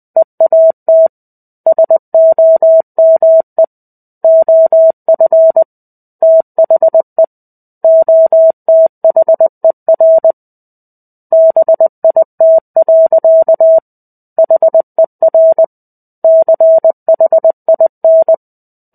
Morse code, heard as a keyed tone, about 20 wpm, 650 Hz. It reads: EAT SOME OF THE OTHER BIT. HER CHIN
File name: 05-Chapter-05 20 WPM